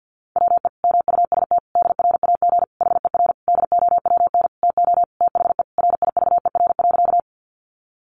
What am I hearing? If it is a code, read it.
Morse code, heard as a keyed tone, about 50 wpm, 700 Hz. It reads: WE MUST DRAG HER DOWN TO THE RIVER.
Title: Sentences from Top 1500 Words 50wpm